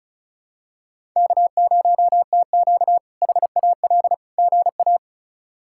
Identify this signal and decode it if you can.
Morse code, heard as a keyed tone, about 34 wpm, 700 Hz. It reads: K0TQ HAL GA
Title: K1USN SST QSO Practice - 35wpm